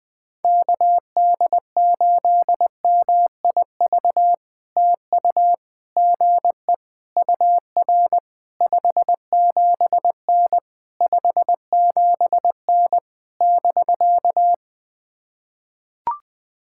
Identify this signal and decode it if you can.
Morse code, heard as a keyed tone, about 19 wpm, 700 Hz. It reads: KD8MIV TU GE UR 57N 57N <BK>
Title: POTA QSO Practice - 20wpm